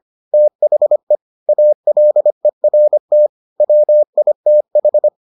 Morse code, heard as a keyed tone, about 25 wpm, 600 Hz. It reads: THE ALERT WITH